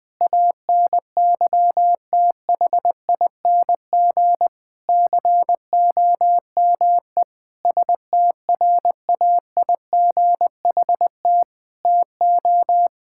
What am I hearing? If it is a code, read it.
Morse code, heard as a keyed tone, about 20 wpm, 700 Hz. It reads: ANYTHING COME STRAIGHT TO
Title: Sentences from Top 1000 Words - Review Part 6 - 20wpm